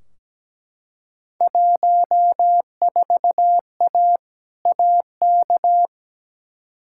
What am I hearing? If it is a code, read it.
Morse code, heard as a keyed tone, about 17 wpm, 700 Hz. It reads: 14A AK
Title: Field Day Exchanges Encore - 17wpm